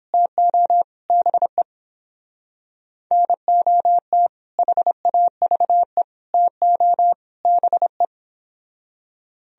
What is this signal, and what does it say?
Morse code, heard as a keyed tone, about 30 wpm, 700 Hz. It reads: TO BE NOT HAVE TO BE